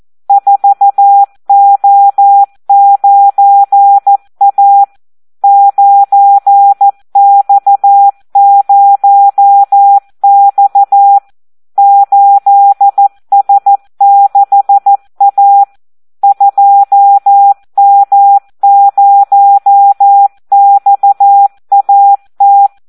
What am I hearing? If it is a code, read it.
Morse code, heard as a keyed tone, about 14 wpm, 800 Hz. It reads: 4O9A 9X0X 8S6A 2M0XAT
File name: Audio practica CW / SD 1 14